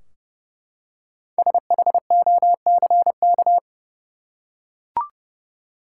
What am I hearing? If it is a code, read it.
Morse code, heard as a keyed tone, about 29 wpm, 700 Hz. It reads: SHOCK
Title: Five Letter Words - 30wpm